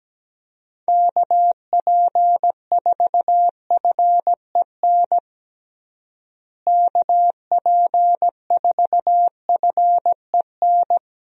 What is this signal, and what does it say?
Morse code, heard as a keyed tone, about 17 wpm, 700 Hz. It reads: KP4FEN KP4FEN
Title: DXCC-100 Call Signs - 6 Characters - Repeated 2x - 17wpm